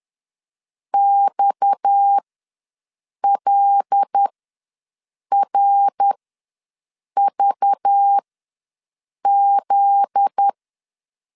Morse code, heard as a keyed tone, about 10 wpm, spaced as for 6 wpm, 800 Hz. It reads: XLRVZ
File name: LEZIONI / 9 28 ws 10wpm 4124